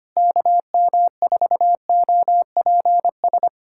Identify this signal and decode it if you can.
Morse code, heard as a keyed tone, about 25 wpm, 700 Hz. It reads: KM4OPS
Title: USA Callsign DE Callsign 25wpm